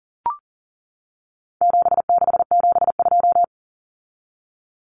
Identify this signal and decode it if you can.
Morse code, heard as a keyed tone, about 38 wpm, 700 Hz. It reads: 7672
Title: Common Cut Numbers - 4-Digit 40wpm